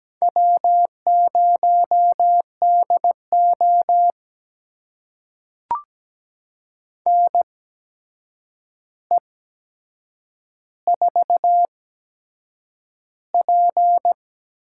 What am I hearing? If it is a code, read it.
Morse code, heard as a keyed tone, about 17 wpm, spaced as for 7 wpm, 700 Hz. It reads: W0DO NE4P
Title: USA Call Signs - 8x Character Spacing - 17wpm